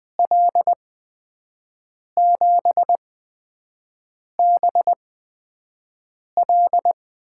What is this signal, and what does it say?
Morse code, heard as a keyed tone, about 20 wpm, spaced as for 5 wpm, 700 Hz. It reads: L7BL
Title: USA Call Signs - 5 Characters - 8x Character Spacing - 20wpm